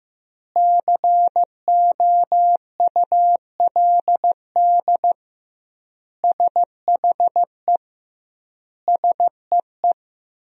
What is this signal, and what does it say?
Morse code, heard as a keyed tone, about 15 wpm, 700 Hz. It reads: COULD SHE SEE
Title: 2x Word-Spacing - Sentences from Top 100 Words - 15wpm